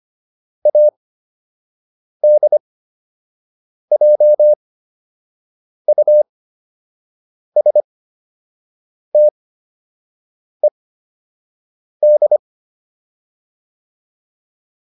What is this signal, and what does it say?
Morse code, heard as a keyed tone, about 25 wpm, spaced as for 6 wpm, 600 Hz. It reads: ADJUSTED
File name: ED8 06